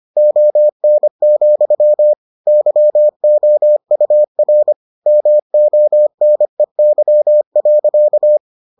Morse code, heard as a keyed tone, about 25 wpm, 600 Hz. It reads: ON, YOUR MONEY.